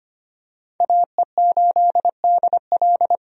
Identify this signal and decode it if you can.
Morse code, heard as a keyed tone, about 25 wpm, 700 Hz. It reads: AE8DL
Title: USA Call Signs - 5 Characters - 25wpm